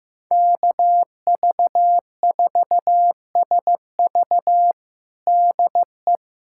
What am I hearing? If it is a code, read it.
Morse code, heard as a keyed tone, about 16 wpm, 700 Hz. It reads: KV4SV DE
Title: USA Callsign DE Callsign - 5 Characters - 15wpm